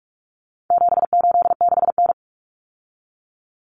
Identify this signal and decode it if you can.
Morse code, heard as a keyed tone, about 44 wpm, 700 Hz. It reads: 786N
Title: Common Cut Numbers - 4-Digit 45wpm